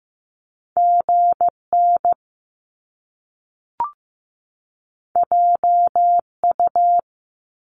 Morse code, heard as a keyed tone, about 15 wpm, 700 Hz. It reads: GN JU
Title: Top Two Letter Combinations in Words 15wpm